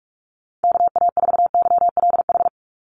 Morse code, heard as a keyed tone, about 44 wpm, 700 Hz. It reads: KA4YLH
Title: USA Call Signs 45wpm